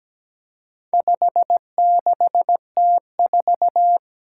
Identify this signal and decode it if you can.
Morse code, heard as a keyed tone, about 17 wpm, 700 Hz. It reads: 56T4
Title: Four Challenging Letter-Number Combinations 17wpm